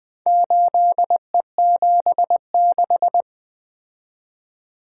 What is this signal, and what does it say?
Morse code, heard as a keyed tone, about 24 wpm, 700 Hz. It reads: 8E76
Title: Speed-Racing - Four Letter-Number Combinations - 20wpm